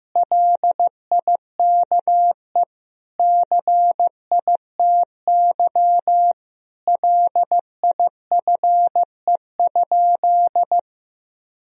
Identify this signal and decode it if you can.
Morse code, heard as a keyed tone, about 16 wpm, 700 Hz. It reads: LIKE CITY LIFE?
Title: Sentences from Top 1000 Words - Review Part 1 - 15wpm